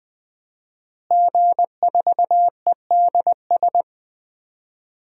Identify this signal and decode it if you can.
Morse code, heard as a keyed tone, about 21 wpm, 700 Hz. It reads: G4EDS
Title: DXCC-100 Call Signs - 5 Characters - Repeated 2x - 20wpm